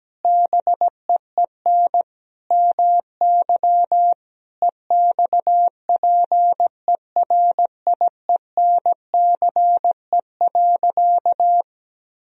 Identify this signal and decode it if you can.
Morse code, heard as a keyed tone, about 17 wpm, 700 Hz. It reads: BEEN MY EXPERIENCE.
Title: Sentences from Top 1000 Words - Review Part 3 - 17wpm